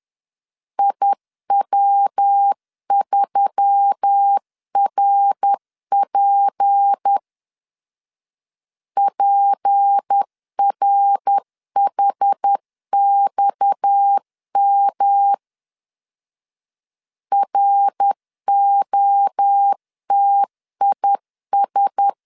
Morse code, heard as a keyed tone, about 10 wpm, 800 Hz. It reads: IW3RP PRHXM ROTIS
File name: CW / DET / 1 6 1500C S C 10wpm 4252